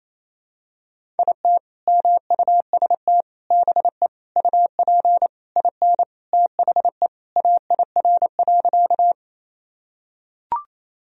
Morse code, heard as a keyed tone, about 28 wpm, 700 Hz. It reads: IT MUST BE UP IN THE AIR.
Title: Sentences from Top 1000 Words - Review Part 1 - 28wpm